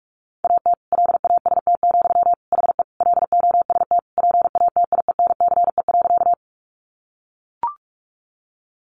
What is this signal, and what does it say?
Morse code, heard as a keyed tone, about 44 wpm, 700 Hz. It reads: AT LAST, HE LOST PATIENCE.
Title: Sentences from Top 1400 Words 45wpm